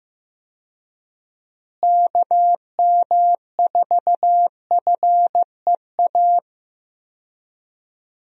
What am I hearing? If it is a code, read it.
Morse code, heard as a keyed tone, about 15 wpm, 700 Hz. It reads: KM4FEA
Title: USA Call Signs - 6 Characters - Repeated 3x - 15wpm